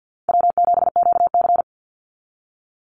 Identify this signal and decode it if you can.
Morse code, heard as a keyed tone, about 50 wpm, 700 Hz. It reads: W7QC
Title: Easy Call Signs - 50wpm